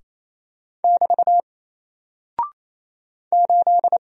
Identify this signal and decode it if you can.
Morse code, heard as a keyed tone, about 27 wpm, 700 Hz. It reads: <BT> 8
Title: Single Character (Letter:Number:Punctuation:Prosign) 28wpm